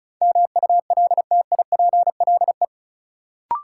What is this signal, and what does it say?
Morse code, heard as a keyed tone, about 34 wpm, 700 Hz. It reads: MULTIPLE
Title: Sets of 5 Words - New Word Drill - 35wpm